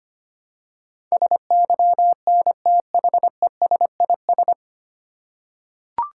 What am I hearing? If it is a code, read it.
Morse code, heard as a keyed tone, about 25 wpm, 700 Hz. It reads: SYNTHESIS
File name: Nine Letter Words - 25wpm